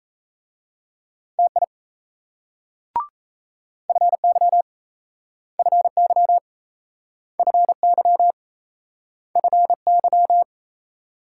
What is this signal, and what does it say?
Morse code, heard as a keyed tone, about 38 wpm, 700 Hz. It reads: TI FY FY FY FY